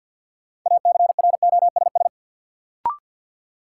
Morse code, heard as a keyed tone, about 53 wpm, 700 Hz. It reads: ACROSS
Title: Sets of 2 Words - New Word Drill - 50wpm